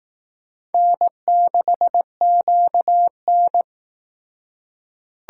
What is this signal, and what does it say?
Morse code, heard as a keyed tone, about 18 wpm, 700 Hz. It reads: N6QN
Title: Speed Racing - USA 1x2 Call Signs 15wpm